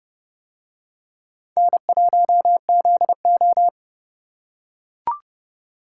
Morse code, heard as a keyed tone, about 29 wpm, 700 Hz. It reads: N1ZO